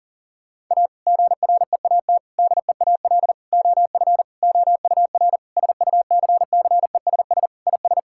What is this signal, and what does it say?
Morse code, heard as a keyed tone, about 40 wpm, 700 Hz. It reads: A GREAT DEAL OF OUR SUCCESS IS
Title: Sentences from Top 1000 Words - Review Part 6 - 40wpm